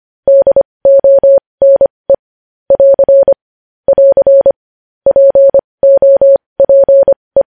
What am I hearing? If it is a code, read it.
Morse code, heard as a keyed tone, about 25 wpm, spaced as for 20 wpm, 550 Hz. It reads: DONE <AR> <AR> POPE
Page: news headlines in morse code at 20 wpm